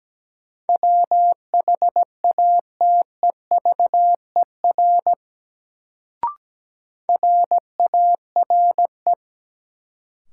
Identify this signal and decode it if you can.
Morse code, heard as a keyed tone, about 17 wpm, 700 Hz. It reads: WHATEVER RARE
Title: Sentences from Top 1100 Words - New Word Drill Alt - 17wpm